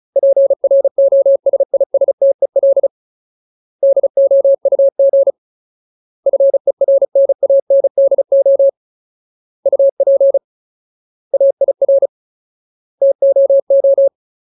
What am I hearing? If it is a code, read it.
Morse code, heard as a keyed tone, about 35 wpm, 550 Hz. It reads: PROSISTEL DOUG FERNANDO UP AIR TOO
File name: Ultimate Word List / 35WPM Ultimate